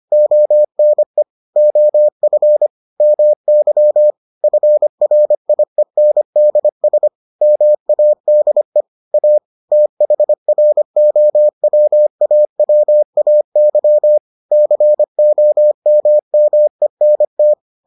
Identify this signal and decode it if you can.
Morse code, heard as a keyed tone, about 25 wpm, 600 Hz. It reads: ONE OF MY FRIENDS MADE A THROWAWAY COMMENT